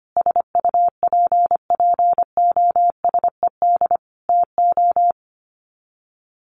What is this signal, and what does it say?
Morse code, heard as a keyed tone, about 25 wpm, 700 Hz. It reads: SUPPOSED TO
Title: Sets of 5 Words Encore2 25wpm